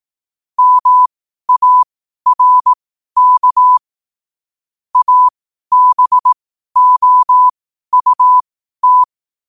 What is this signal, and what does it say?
Morse code, heard as a keyed tone, about 18 wpm, spaced as for 12 wpm, 1000 Hz. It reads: MARK ABOUT